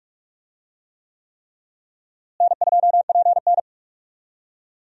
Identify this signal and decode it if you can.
Morse code, heard as a keyed tone, about 46 wpm, 700 Hz. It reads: N2PN
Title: USA Call Signs - 4 Characters - Repeated 3x - 45wpm